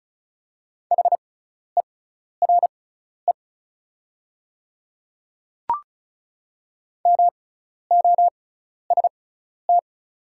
Morse code, heard as a keyed tone, about 36 wpm, spaced as for 12 wpm, 690 Hz. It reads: HERE MOST
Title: 6x Character Spacing - Top 200 Words - 35wpm